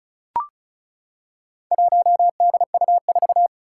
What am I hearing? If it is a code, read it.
Morse code, heard as a keyed tone, about 34 wpm, 700 Hz. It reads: 1DU4
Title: Four Challenging Letter-Number Combinations 35wpm